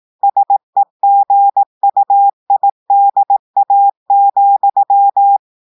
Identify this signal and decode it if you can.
Morse code, heard as a keyed tone, about 18 wpm, 800 Hz. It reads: SEGUIDA,
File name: audios / portugais / portugais texte en clair 1 vitesse 18 mots minutes